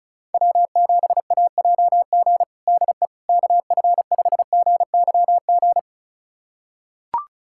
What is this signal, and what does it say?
Morse code, heard as a keyed tone, about 36 wpm, 700 Hz. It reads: W7AJG DE KF5GYG
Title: USA Callsign DE Callsign 35wpm